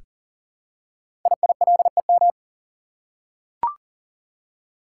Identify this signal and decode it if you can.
Morse code, heard as a keyed tone, about 38 wpm, 700 Hz. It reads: IILEM
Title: Code Groups - 5 Letter:Numbers - 40wpm